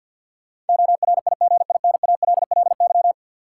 Morse code, heard as a keyed tone, about 52 wpm, 700 Hz. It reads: ORIGINALLY
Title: Sentences from Top 1500 Words - New Word Drill - 50wpm